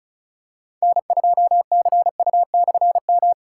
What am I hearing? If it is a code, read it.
Morse code, heard as a keyed tone, about 36 wpm, 700 Hz. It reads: N2CU/M
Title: K1USN SST QSO Practice - 35wpm